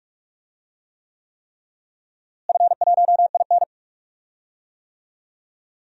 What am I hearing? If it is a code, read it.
Morse code, heard as a keyed tone, about 41 wpm, 700 Hz. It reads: F1IN
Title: DXCC-100 Call Signs - 4 Characters - Repeated 3x - 45wpm